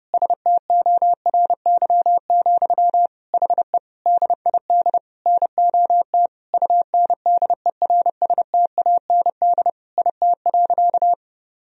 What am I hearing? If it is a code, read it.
Morse code, heard as a keyed tone, about 30 wpm, 700 Hz. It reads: STORY, HE DID NOT UNDERSTAND IT.
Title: Sentences from Top 1000 Words - Review Part 3 - 30wpm